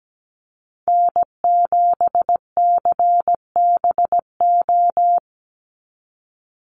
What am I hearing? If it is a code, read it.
Morse code, heard as a keyed tone, about 17 wpm, 700 Hz. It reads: N7CBO
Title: USA Call Signs 17wpm